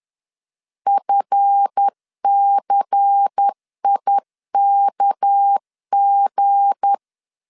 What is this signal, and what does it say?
Morse code, heard as a keyed tone, about 10 wpm, 800 Hz. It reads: FCIKG
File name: LEZIONI / 8 YJ45 / 8 YJ45 10wpm 2549